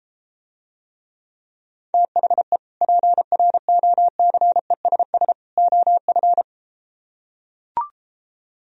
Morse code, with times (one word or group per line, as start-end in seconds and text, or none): 1.9-2.6 THE
2.8-5.3 PROCESS
5.6-6.4 OF